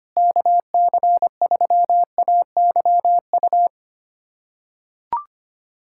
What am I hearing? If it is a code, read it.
Morse code, heard as a keyed tone, about 25 wpm, 700 Hz. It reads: KC3AYU
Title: USA Callsign DE Callsign 25wpm